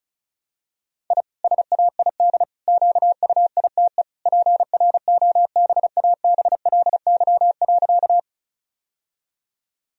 Morse code, heard as a keyed tone, about 36 wpm, 700 Hz. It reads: I SAID QUITE PROBABLY.
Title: Sentences from Top 1000 Words - Review Part 4 - 35wpm